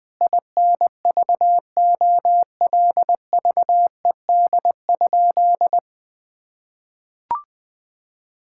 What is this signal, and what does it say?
Morse code, heard as a keyed tone, about 20 wpm, 700 Hz. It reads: INVOLVED?
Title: Sentences from Top 1000 Words - Review Part 5 - 20wpm